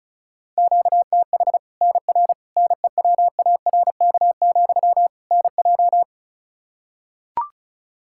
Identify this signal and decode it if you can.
Morse code, heard as a keyed tone, about 36 wpm, 700 Hz. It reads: QTH NR NEWARK, NJ
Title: Practice QSO - Exchange 1 - Beginning - 35wpm